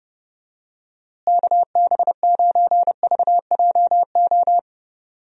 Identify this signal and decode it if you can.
Morse code, heard as a keyed tone, about 30 wpm, 700 Hz. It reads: KB9VJO
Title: USA Call Signs - 6 Characters - Repeated 3x - 30wpm